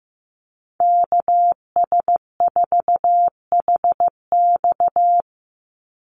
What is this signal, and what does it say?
Morse code, heard as a keyed tone, about 15 wpm, 700 Hz. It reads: KS4HX
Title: USA Call Signs 15wpm